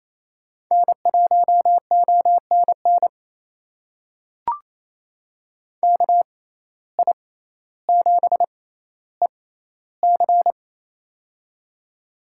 Morse code, heard as a keyed tone, about 28 wpm, spaced as for 14 wpm, 700 Hz. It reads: N1ONN KI7EC
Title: USA Call Signs - 5 Characters - 6x Character Spacing - 28wpm